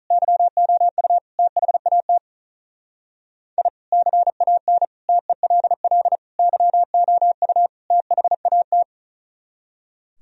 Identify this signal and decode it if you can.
Morse code, heard as a keyed tone, about 44 wpm, 700 Hz. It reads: YOU THAT I CAN TELL YOU THAT
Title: Speed-Racing - Sets of 5 Words 35wpm